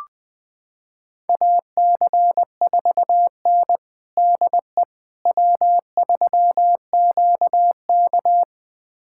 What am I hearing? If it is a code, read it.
Morse code, heard as a keyed tone, about 20 wpm, 700 Hz. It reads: AC4N DE W3QK
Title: USA Callsign DE Callsign - 4 Characters - 20wpm